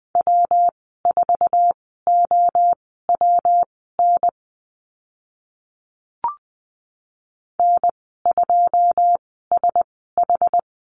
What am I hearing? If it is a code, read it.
Morse code, heard as a keyed tone, about 20 wpm, spaced as for 15 wpm, 700 Hz. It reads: W4OWN N2SH
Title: 154 Lesson 25g - Letter p - Call Signs 20wpm